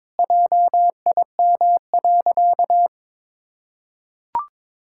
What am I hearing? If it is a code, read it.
Morse code, heard as a keyed tone, about 21 wpm, 700 Hz. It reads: JIM.
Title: Pangram Challenge v2 - 22wpm